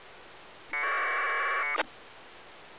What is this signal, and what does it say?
Packet radio (AX.25, as in APRS).